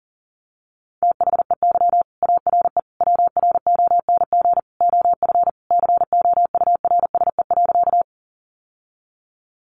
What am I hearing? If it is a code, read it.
Morse code, heard as a keyed tone, about 40 wpm, 700 Hz. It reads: THEY ARE WRONG OF COURSE.
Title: Sentences from Top 900 Words 40wpm